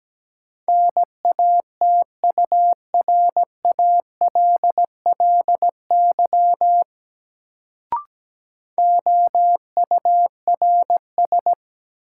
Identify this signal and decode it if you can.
Morse code, heard as a keyed tone, about 17 wpm, 700 Hz. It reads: NATURALLY OURS
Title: Sentences from Top 1300 Words - New Word Drill Alt - 17wpm